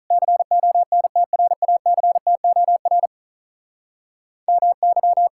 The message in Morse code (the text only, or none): CONTRACTOR MY